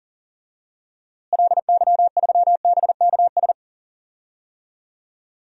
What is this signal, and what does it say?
Morse code, heard as a keyed tone, about 40 wpm, 700 Hz. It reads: LY3BKS
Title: DXCC-100 Call Signs - 6 Characters - Repeated 3x - 40wpm